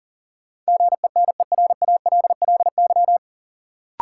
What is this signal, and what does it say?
Morse code, heard as a keyed tone, about 40 wpm, 700 Hz. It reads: GENERALLY
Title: Sentences from Top 600 Words - New Word Drill - 40wpm